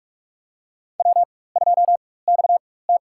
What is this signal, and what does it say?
Morse code, heard as a keyed tone, about 43 wpm, spaced as for 21 wpm, 700 Hz. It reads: W2XT
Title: USA Call Signs - 4 Characters - 4x Character Spacing - 45wpm